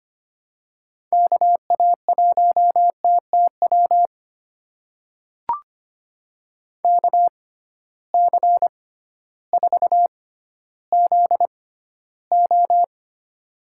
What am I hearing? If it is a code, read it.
Morse code, heard as a keyed tone, about 25 wpm, spaced as for 14 wpm, 700 Hz. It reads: KA1TTW KC4ZO
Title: USA Call Signs - 6 Characters - 6x Character Spacing - 25wpm